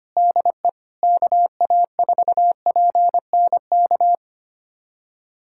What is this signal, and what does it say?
Morse code, heard as a keyed tone, about 25 wpm, 700 Hz. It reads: DE KA4PNK
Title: USA Callsign DE Callsign 25wpm